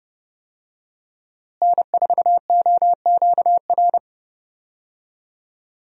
Morse code, heard as a keyed tone, about 30 wpm, 700 Hz. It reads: N4OQR
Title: USA Call Signs - 5 Characters - Repeated 2x - 30wpm